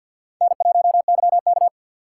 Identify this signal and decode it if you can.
Morse code, heard as a keyed tone, about 52 wpm, 700 Hz. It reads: N1YK
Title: USA Call Signs - 4x Character Spacing - 50wpm